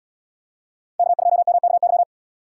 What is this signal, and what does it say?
Morse code, heard as a keyed tone, about 77 wpm, 700 Hz. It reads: K2MKY